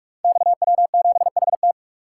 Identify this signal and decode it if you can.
Morse code, heard as a keyed tone, about 44 wpm, 690 Hz. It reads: XW7HT
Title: Code Groups - 5 Letter:Numbers - 45wpm